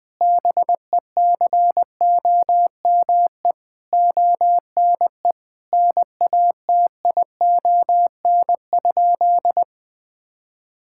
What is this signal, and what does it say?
Morse code, heard as a keyed tone, about 20 wpm, 700 Hz. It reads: BECOME ONE NATION?
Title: Sentences from Top 1000 Words - Review Part 6 - 20wpm